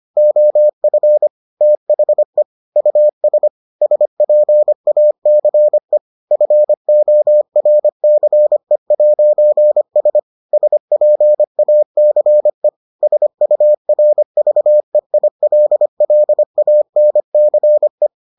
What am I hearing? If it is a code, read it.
Morse code, heard as a keyed tone, about 25 wpm, 600 Hz. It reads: OF THE US SPACE FORCE'S SPACE SURVEILLANCE